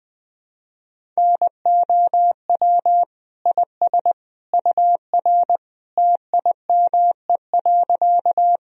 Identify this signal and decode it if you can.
Morse code, heard as a keyed tone, about 21 wpm, 700 Hz. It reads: NOW IS UR TIME.
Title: Sentences from Top 100 Words and Abbreviations 20wpm